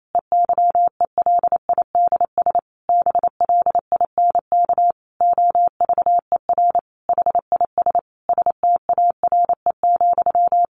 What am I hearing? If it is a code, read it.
Morse code, heard as a keyed tone, about 28 wpm, 700 Hz. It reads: EYELIDS BLINK OVER HIS STARE,
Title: A Sound Of Thunder 28wpm